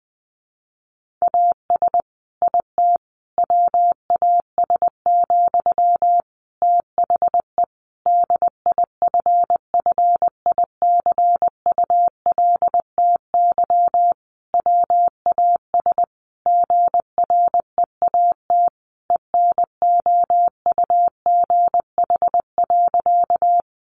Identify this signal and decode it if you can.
Morse code, heard as a keyed tone, about 20 wpm, 700 Hz. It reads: AS IT WAS, THE DIFFICULTY WAS GREAT ENOUGH.
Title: Sentences from Top 1500 Words 20wpm